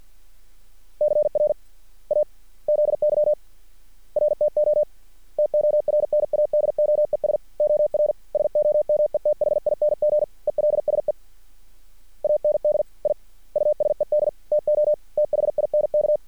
Morse code, heard as a keyed tone, about 50 wpm, spaced as for 43 wpm, 600 Hz. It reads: CRAZY RTO TORNADOES OR SOMETHING ELSE AND I USED TO THINK
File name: archived / CRSnet